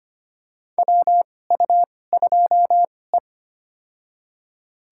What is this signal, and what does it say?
Morse code, heard as a keyed tone, about 25 wpm, spaced as for 18 wpm, 700 Hz. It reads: WU2E
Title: USA Call Signs - 4 Characters - 2x Character Spacing - 25wpm